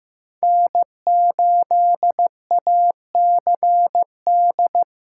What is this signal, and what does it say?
Morse code, heard as a keyed tone, about 15 wpm, 700 Hz. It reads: N8ACD
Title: USA Call Signs - 5 Characters - 6x Character Spacing - 15wpm